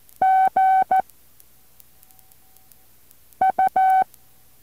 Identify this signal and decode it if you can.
Morse code, heard as a keyed tone, about 14 wpm, 760 Hz. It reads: G U
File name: intl morse code2